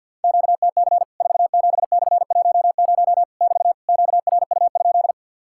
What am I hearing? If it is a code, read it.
Morse code, heard as a keyed tone, about 50 wpm, 700 Hz. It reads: QTC 47/10 <BT> QRU?
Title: Worked All Europe DX Contest - QTC Exchange with Repeats - 50wpm